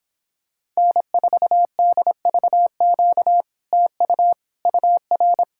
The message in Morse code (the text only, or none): N4DVQ TU UR